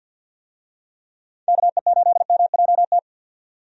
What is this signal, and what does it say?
Morse code, heard as a keyed tone, about 50 wpm, 700 Hz. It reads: KE8MJT
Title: USA Call Signs - 6 Characters - 50wpm